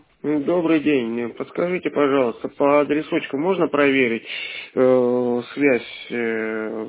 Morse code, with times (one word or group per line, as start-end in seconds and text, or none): none